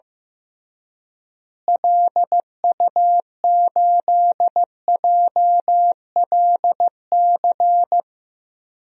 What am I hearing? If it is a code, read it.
Morse code, heard as a keyed tone, about 15 wpm, 700 Hz. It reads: LU8JLC
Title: DXCC-100 Call Signs - 6 Characters - Repeated 2x - 15wpm